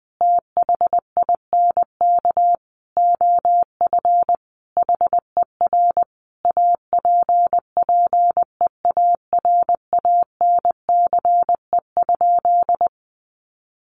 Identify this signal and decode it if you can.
Morse code, heard as a keyed tone, about 19 wpm, 700 Hz. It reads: THINK OF HER APPEARANCE?
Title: Sentences from Top 1100 Words 20wpm